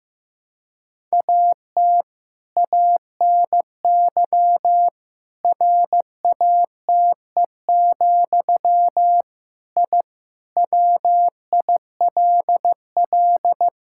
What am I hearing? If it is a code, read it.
Morse code, heard as a keyed tone, about 15 wpm, 700 Hz. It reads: AT ANY RATE, I WILL
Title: Sentences from Top 1000 Words - Review Part 6 - 15wpm